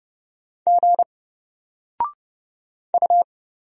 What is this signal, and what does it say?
Morse code, heard as a keyed tone, about 30 wpm, 700 Hz. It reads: G U
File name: Single-Letter-Number-30wpm